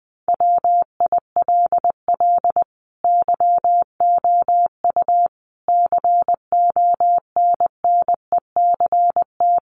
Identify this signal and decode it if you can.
Morse code, heard as a keyed tone, about 20 wpm, 700 Hz. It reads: WILL YOU CONNECT
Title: Sentences from Top 1500 Words 20wpm